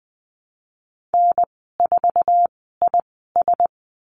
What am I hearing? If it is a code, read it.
Morse code, heard as a keyed tone, about 20 wpm, spaced as for 15 wpm, 700 Hz. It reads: N4IS
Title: 076 Lesson 13g - Letter L - Call Signs 20wpm